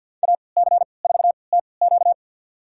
Morse code, heard as a keyed tone, about 47 wpm, spaced as for 28 wpm, 700 Hz. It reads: AC4TQ
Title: USA Call Signs - 5 Characters - 3x Character Spacing - 50wpm